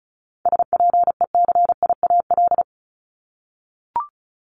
Morse code, heard as a keyed tone, about 36 wpm, 700 Hz. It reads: SPECIAL
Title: Multi-letter Phonemes 35wpm